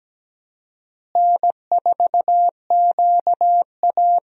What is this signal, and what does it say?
Morse code, heard as a keyed tone, about 17 wpm, 700 Hz. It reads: N4QA
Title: USA Call Signs - 4 Characters - Repeated 3x - 17wpm